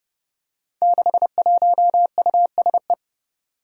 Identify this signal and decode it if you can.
Morse code, heard as a keyed tone, about 31 wpm, 700 Hz. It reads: 61USE
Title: Five Challenging Letter-Number Combinations 30wpm